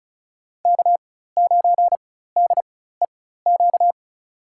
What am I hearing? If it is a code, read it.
Morse code, heard as a keyed tone, about 34 wpm, spaced as for 16 wpm, 700 Hz. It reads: K9DEQ
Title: USA Call Signs - 5 Characters - 4x Character Spacing - 35wpm